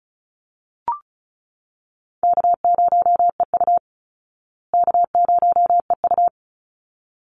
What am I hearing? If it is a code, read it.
Morse code, heard as a keyed tone, about 34 wpm, 700 Hz. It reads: K0EU K0EU
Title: Easy Call Signs Repeated 2x - 35wpm